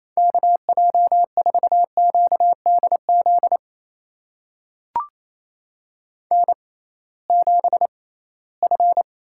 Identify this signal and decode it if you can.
Morse code, heard as a keyed tone, about 28 wpm, spaced as for 18 wpm, 700 Hz. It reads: KJ4QDZ N7F
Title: USA Call Signs - 6x Character Spacing - 28wpm